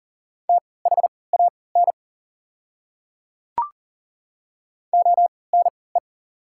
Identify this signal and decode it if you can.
Morse code, heard as a keyed tone, about 37 wpm, spaced as for 22 wpm, 690 Hz. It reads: THAN ONE
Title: 3x Character Spacing - Top 100 Words - 40wpm